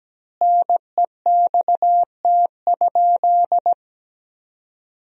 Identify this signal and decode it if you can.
Morse code, heard as a keyed tone, about 17 wpm, 700 Hz. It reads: NEXT?
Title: 3x Word-Spacing - Sentences from Top 300 Words - 17wpm